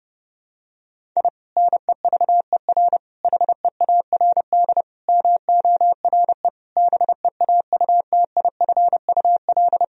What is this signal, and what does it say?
Morse code, heard as a keyed tone, about 30 wpm, 700 Hz. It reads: I NEVER HEARD MORE BEAUTIFUL
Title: Sentences from Top 1000 Words - Review Part 1 - 30wpm